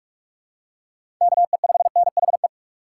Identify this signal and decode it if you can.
Morse code, heard as a keyed tone, about 45 wpm, 700 Hz. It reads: KE5NHE